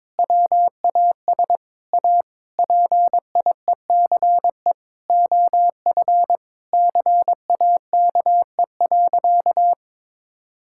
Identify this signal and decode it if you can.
Morse code, heard as a keyed tone, about 22 wpm, 700 Hz. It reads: WAS A PIECE OF CAKE.
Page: morse-code-ninja-practice-22wpm